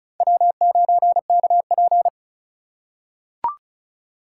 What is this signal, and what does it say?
Morse code, heard as a keyed tone, about 34 wpm, 700 Hz. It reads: W9KP
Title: USA Call Signs - 4 Characters - 3x Character Spacing - 35wpm